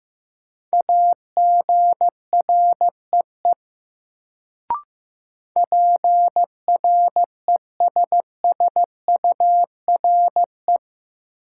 Morse code, heard as a keyed tone, about 15 wpm, 700 Hz. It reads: AGREE PRESSURE
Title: Sentences from Top 1000 Words - Review Part 6 - New Word Drill - 15wpm